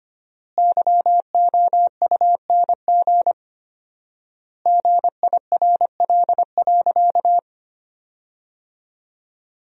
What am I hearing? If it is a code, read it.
Morse code, heard as a keyed tone, about 25 wpm, 700 Hz. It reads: YOUNG GIRL.
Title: 4x Word-Spacing - Sentences from Top 400 Words - 25wpm